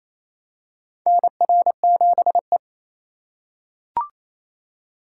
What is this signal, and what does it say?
Morse code, heard as a keyed tone, about 28 wpm, 700 Hz. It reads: NR7E
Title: USA Call Signs - 4 Characters - 3x Character Spacing - 28wpm